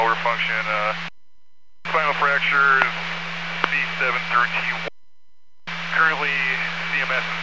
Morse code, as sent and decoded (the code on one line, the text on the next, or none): none